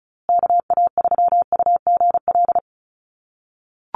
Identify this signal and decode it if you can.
Morse code, heard as a keyed tone, about 34 wpm, 700 Hz. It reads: KA3UGL
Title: USA Call Signs Repeated 2x 35wpm